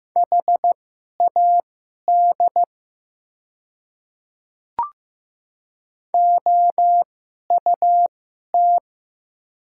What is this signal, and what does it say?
Morse code, heard as a keyed tone, about 15 wpm, spaced as for 11 wpm, 700 Hz. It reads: HAD OUT
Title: 2x Character Spacing - Top 200 Words - 15wpm